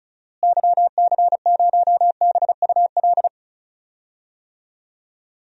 Morse code, heard as a keyed tone, about 36 wpm, 700 Hz. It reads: YC0BUL
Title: DXCC-100 Call Signs - 6 Characters - Repeated 3x - 35wpm